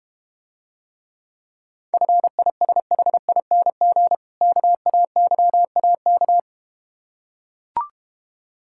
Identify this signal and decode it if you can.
Morse code, heard as a keyed tone, about 32 wpm, 700 Hz. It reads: FISHING KAYAK